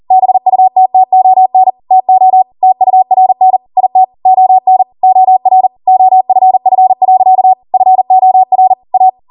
Morse code, heard as a keyed tone, about 40 wpm, 750 Hz. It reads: BUTTON TO TURN IT ON OR OFF. FOR A
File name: Morse / 40WPM